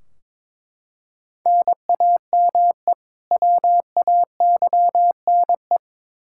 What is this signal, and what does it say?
Morse code, heard as a keyed tone, about 22 wpm, 700 Hz. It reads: NAME WAYNE
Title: Practice QSO - Exchange 1 - Beginning - 22wpm